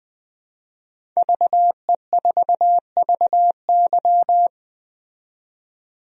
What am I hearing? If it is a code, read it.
Morse code, heard as a keyed tone, about 21 wpm, 700 Hz. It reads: VE4VY
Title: DXCC-100 Call Signs - 5 Characters - 20wpm